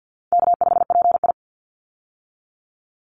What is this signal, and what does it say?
Morse code, heard as a keyed tone, about 49 wpm, 700 Hz. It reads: K5PI
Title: Easy Call Signs Repeated 2x - 50wpm